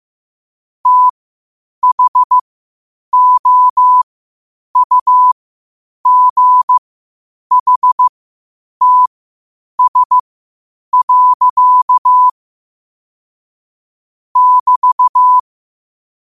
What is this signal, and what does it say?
Morse code, heard as a keyed tone, about 15 wpm, spaced as for 8 wpm, 1000 Hz. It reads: THOUGHTS. <BT>
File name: qotdfiles / 8wpm